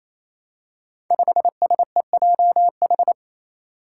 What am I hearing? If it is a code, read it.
Morse code, heard as a keyed tone, about 28 wpm, 700 Hz. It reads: 5SEJH
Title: Five Challenging Letter-Number Combinations 28wpm